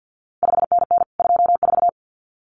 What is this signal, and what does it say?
Morse code, heard as a keyed tone, about 50 wpm, 700 Hz. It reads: 5NN 24
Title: CQ WW DX Contest Exchange - 50wpm